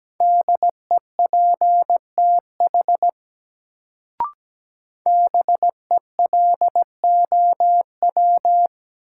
Morse code, heard as a keyed tone, about 17 wpm, 700 Hz. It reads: DEPTH BELOW
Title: Sentences from Top 1200 Words - New Word Drill Alt - 17wpm